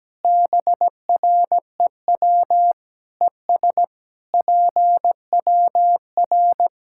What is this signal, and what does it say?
Morse code, heard as a keyed tone, about 17 wpm, 700 Hz. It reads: BREW ES PWR
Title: Practice QSO - Exchange 2 - Info - 17wpm